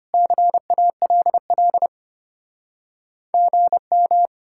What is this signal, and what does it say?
Morse code, heard as a keyed tone, about 33 wpm, 700 Hz. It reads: CALL GM